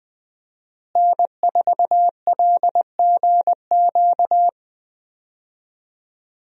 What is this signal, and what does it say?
Morse code, heard as a keyed tone, about 20 wpm, 700 Hz. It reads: N4LGQ
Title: USA Call Signs - 5 Characters - Repeated 2x - 20wpm